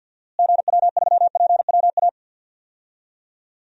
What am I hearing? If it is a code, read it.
Morse code, heard as a keyed tone, about 53 wpm, 700 Hz. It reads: GW3PWA
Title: DXCC-100 Call Signs - 6 Characters - Repeated 3x - 50wpm